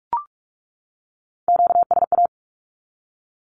Morse code, heard as a keyed tone, about 44 wpm, 700 Hz. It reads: QSA
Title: Q-Codes 45wpm